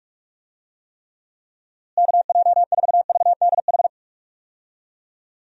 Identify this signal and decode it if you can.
Morse code, heard as a keyed tone, about 46 wpm, 700 Hz. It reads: KJ4VDH
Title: USA Call Signs - 6 Characters - Repeated 2x - 45wpm